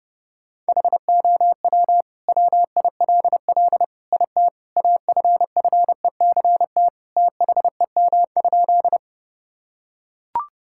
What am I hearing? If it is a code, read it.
Morse code, heard as a keyed tone, about 32 wpm, 700 Hz. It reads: HOW WILL IT AFFECT THEM?
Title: Sentences from Top 1000 Words - Review Part 6 - 30wpm